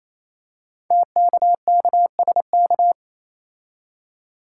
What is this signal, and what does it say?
Morse code, heard as a keyed tone, about 27 wpm, 700 Hz. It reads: TKKSK
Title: Code Groups - 5 Letters - 28wpm